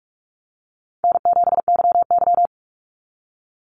Morse code, heard as a keyed tone, about 46 wpm, 700 Hz. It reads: N7YY
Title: NA SOTA Chaser Callsigns 45wpm